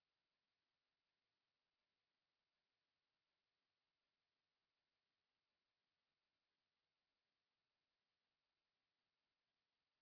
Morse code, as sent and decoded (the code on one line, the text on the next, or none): none